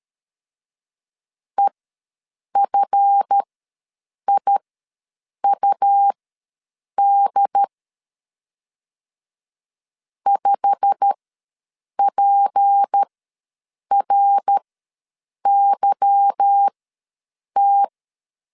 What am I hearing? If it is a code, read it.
Morse code, heard as a keyed tone, about 13 wpm, spaced as for 7 wpm, 800 Hz. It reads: EFIUD 5PRYT
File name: LEZIONI / 8 YJ45 / 8 YJ45 ws 12wpm 3146